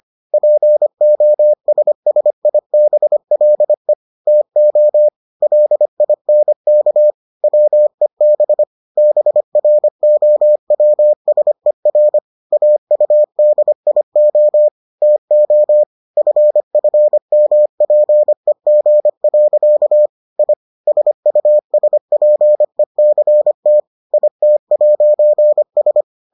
Morse code, heard as a keyed tone, about 25 wpm, 600 Hz. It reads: POSSIBLE TO LINK WEB BROWSER AUDIO TO FFMPEG. I SUSPECT IT'S